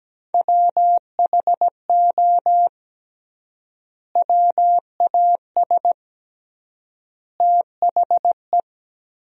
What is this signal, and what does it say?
Morse code, heard as a keyed tone, about 17 wpm, 700 Hz. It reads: WHO WAS THE
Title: 3x Word-Spacing - Sentences from Top 300 Words - 17wpm